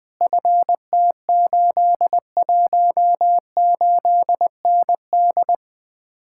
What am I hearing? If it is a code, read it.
Morse code, heard as a keyed tone, about 20 wpm, 700 Hz. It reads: FT818ND
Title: Speed-Racing Common QSO Phrases - 17wpm